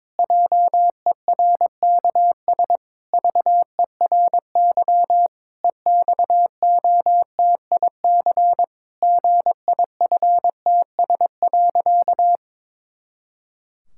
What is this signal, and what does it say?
Morse code, heard as a keyed tone, about 22 wpm, 700 Hz. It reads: JERKS VERY EXOTIC GIFTS.
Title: Pangram Challenge v2 - 22wpm